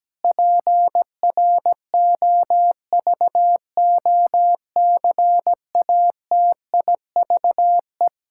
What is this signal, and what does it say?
Morse code, heard as a keyed tone, about 17 wpm, 700 Hz. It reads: PROVOCATIVE
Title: Pangram Challenge v2 - 17wpm